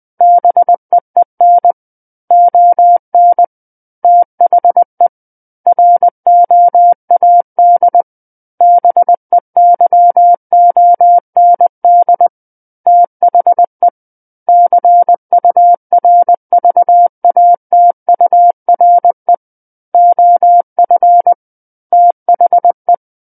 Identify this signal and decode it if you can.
Morse code, heard as a keyed tone, about 20 wpm, 700 Hz. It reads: BEEN ON THE ROAD BEYOND THE CURVATURE OF THE
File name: 20 WPM